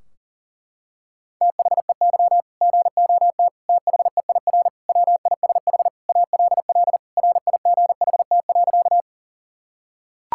Morse code, heard as a keyed tone, about 40 wpm, 700 Hz. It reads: THEY GOT THEIR WISH ALL RIGHT.
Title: Sentences from Top 1600 Words 40wpm